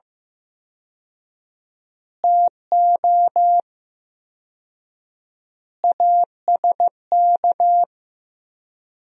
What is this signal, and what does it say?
Morse code, heard as a keyed tone, about 15 wpm, 700 Hz. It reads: TO ASK
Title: 4x Word-Spacing - Sentences from Top 200 Words - 15wpm